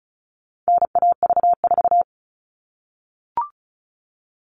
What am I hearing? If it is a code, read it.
Morse code, heard as a keyed tone, about 34 wpm, 700 Hz. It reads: NAV4
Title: All Cut Numbers - 4-Digit 35wpm